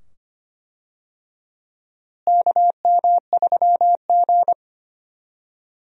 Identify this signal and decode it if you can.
Morse code, heard as a keyed tone, about 25 wpm, 700 Hz. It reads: KM3G
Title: USA Call Signs - 4 Characters - Repeated 2x - 25wpm